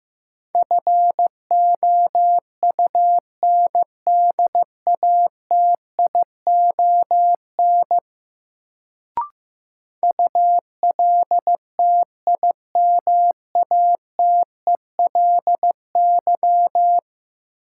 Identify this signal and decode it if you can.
Morse code, heard as a keyed tone, about 16 wpm, 700 Hz. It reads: FOUNDATION ULTIMATELY
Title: Sentences from Top 1200 Words - New Word Drill Alt - 15wpm